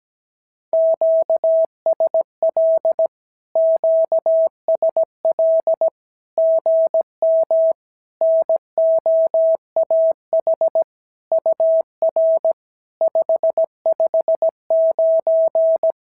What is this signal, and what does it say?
Morse code, heard as a keyed tone, about 17 wpm, 690 Hz. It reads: QSL QSL GM NOAH UR 559
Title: SOTA QSO Practice 17wpm